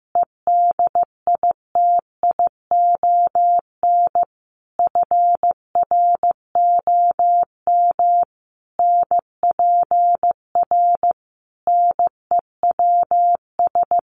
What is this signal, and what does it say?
Morse code, heard as a keyed tone, about 14 wpm, 700 Hz. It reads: EDITION FROM NPR NEWS
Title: Sets of 5 Words Encore 15wpm